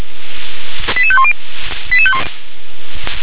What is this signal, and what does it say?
Five-tone selective call (ZVEI), heard as quick runs of five tones, tones of about 70 ms. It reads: 88510